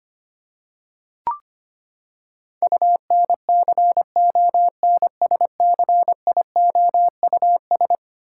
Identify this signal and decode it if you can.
Morse code, heard as a keyed tone, about 25 wpm, 700 Hz. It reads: UNCONSCIOUS
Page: morse-code-ninja-practice-25wpm